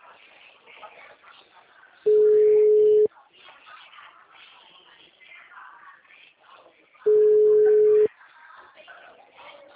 Ringback tone (Europe, 425 Hz).